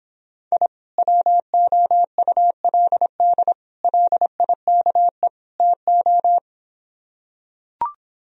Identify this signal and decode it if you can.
Morse code, heard as a keyed tone, about 26 wpm, 700 Hz. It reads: I WOULD LIKE TO